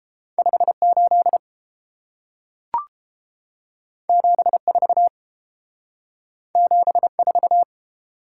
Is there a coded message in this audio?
Yes: Morse code from 0.4 to 7.6 s, heard as a keyed tone, about 30 wpm, 700 Hz. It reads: 58 74 74